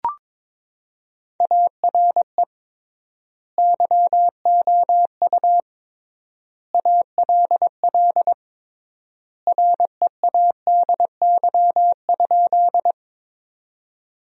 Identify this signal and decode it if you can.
Morse code, heard as a keyed tone, about 22 wpm, 700 Hz. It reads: ARE YOU ALL READY?
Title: 3x Word-Spacing - Sentences from Top 400 Words - 22wpm